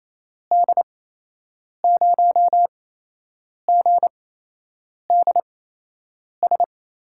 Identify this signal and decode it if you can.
Morse code, heard as a keyed tone, about 28 wpm, spaced as for 8 wpm, 700 Hz. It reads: D0GDS
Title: USA Call Signs - 6 Characters - 8x Character Spacing - 28wpm